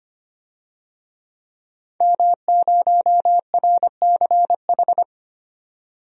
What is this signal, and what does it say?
Morse code, heard as a keyed tone, about 25 wpm, 700 Hz. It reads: M0RCH
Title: DXCC-100 Call Signs - 5 Characters - 25wpm